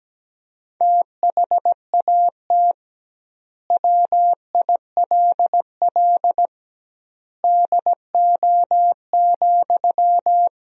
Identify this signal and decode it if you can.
Morse code, heard as a keyed tone, about 17 wpm, 700 Hz. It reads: THAT WILL DO,
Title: 2x Word-Spacing - Sentences from Top 100 Words - 17wpm